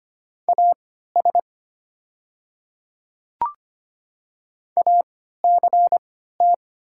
Morse code, heard as a keyed tone, about 25 wpm, spaced as for 14 wpm, 700 Hz. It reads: AS ACT
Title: 3x Character Spacing - Top 500 Words - 25wpm